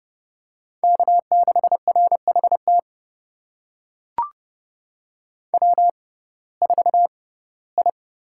Morse code, heard as a keyed tone, about 32 wpm, spaced as for 19 wpm, 700 Hz. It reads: K6RHT W4I
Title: USA Call Signs - 5 Characters - 6x Character Spacing - 30wpm